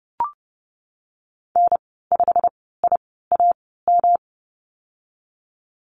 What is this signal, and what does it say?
Morse code, heard as a keyed tone, about 30 wpm, spaced as for 17 wpm, 700 Hz. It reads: N5IAM
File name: Lesson 19g - Letter w - Call Signs 30wpm